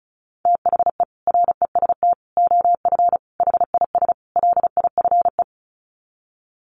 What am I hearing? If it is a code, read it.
Morse code, heard as a keyed tone, about 34 wpm, 700 Hz. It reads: THE REST OF HIS LIFE
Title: Sets of 5 Words 35wpm